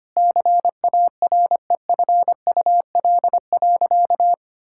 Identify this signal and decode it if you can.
Morse code, heard as a keyed tone, about 25 wpm, 700 Hz. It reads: CAREFUL.
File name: practice / 25wpm / Sentences with 3-Word Phrases - 25wpm